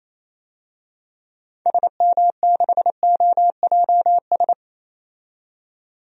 Morse code, heard as a keyed tone, about 28 wpm, 700 Hz. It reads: SM6OJS
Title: DXCC-100 Call Signs - 6 Characters - 28wpm